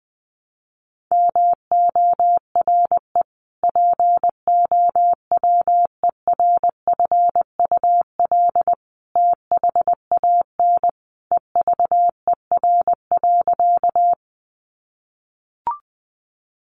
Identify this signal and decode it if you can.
Morse code, heard as a keyed tone, about 20 wpm, 700 Hz. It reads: MORE POWERFUL THAN EVER.
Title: Sentences from Top 1300 Words 20wpm